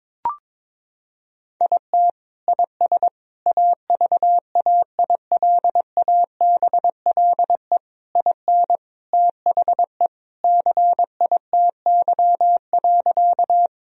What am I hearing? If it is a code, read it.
Morse code, heard as a keyed tone, about 22 wpm, 700 Hz. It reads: IT IS AVAILABLE IN THE CITY.
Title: Sentences from Top 1000 Words - Review Part 3 - 22wpm